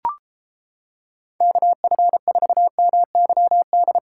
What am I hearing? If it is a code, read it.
Morse code, heard as a keyed tone, about 29 wpm, 700 Hz. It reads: KF4MYD